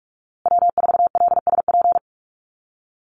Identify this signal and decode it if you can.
Morse code, heard as a keyed tone, about 44 wpm, 700 Hz. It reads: W4LSP
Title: USA Call Signs Repeated 2x 45wpm